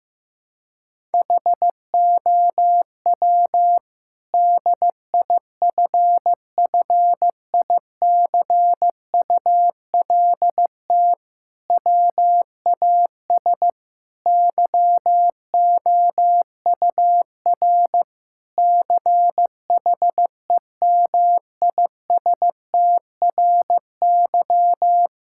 Morse code, heard as a keyed tone, about 15 wpm, 700 Hz. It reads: HOW DIFFICULT WAS YOUR CHEMISTRY
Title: Sentences from Top 1000 Words - Review Part 6 - 15wpm